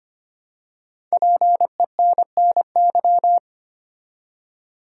Morse code, heard as a keyed tone, about 25 wpm, 700 Hz. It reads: PENNY
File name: 5 Letter Words - First 3 Letters Given - 25wpm